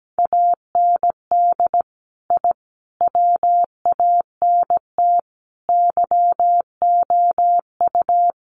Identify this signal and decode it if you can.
Morse code, heard as a keyed tone, about 17 wpm, 700 Hz. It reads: AND I WANT YOU
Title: Sets of 5 Words Encore 17wpm